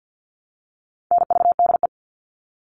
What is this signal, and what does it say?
Morse code, heard as a keyed tone, about 48 wpm, 700 Hz. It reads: NVDE
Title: All Cut Numbers - 4-Digit 50wpm